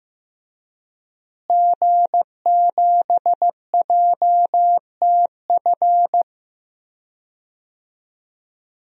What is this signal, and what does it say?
Morse code, heard as a keyed tone, about 15 wpm, 700 Hz. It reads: G7JTF
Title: DXCC-100 Call Signs - 5 Characters - Repeated 3x - 15wpm